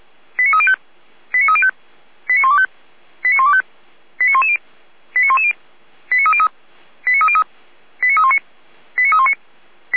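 Five-tone selective call (ZVEI), heard as quick runs of five tones, tones of about 70 ms. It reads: 89305 89305 89125 89125 89110 89110 89303 89303 89319 89319
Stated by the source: ZVEI five-tone selective call 89042, received on SDR.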